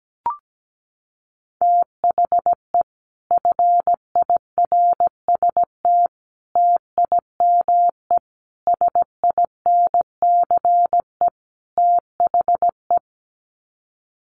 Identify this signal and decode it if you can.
Morse code, heard as a keyed tone, about 17 wpm, 700 Hz. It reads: THE FIRST TIME SINCE THE
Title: Sets of 5 Words Encore 17wpm